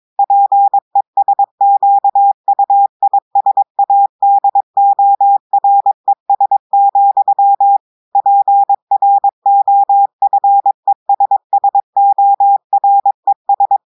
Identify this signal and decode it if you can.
Morse code, heard as a keyed tone, about 22 wpm, 800 Hz. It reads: PESQUISADORES, PROFESSORES
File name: audios / portugais / portugais texte en clair 1 vitesse 22 mots minutes